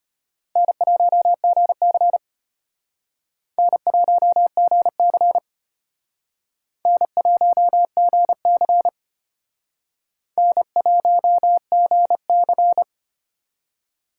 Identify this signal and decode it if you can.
Morse code, heard as a keyed tone, about 32 wpm, 700 Hz. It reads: N1GC N1GC N1GC N1GC